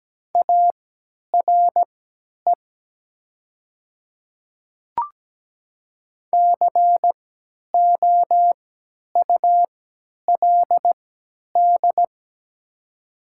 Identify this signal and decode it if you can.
Morse code, heard as a keyed tone, about 17 wpm, spaced as for 10 wpm, 690 Hz. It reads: ARE COULD
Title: 3x Character Spacing - Top 100 Words - 17wpm